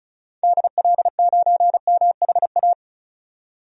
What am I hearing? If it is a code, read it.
Morse code, heard as a keyed tone, about 36 wpm, 700 Hz. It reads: DL9MHA
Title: DXCC-100 Call Signs - 6 Characters - 35wpm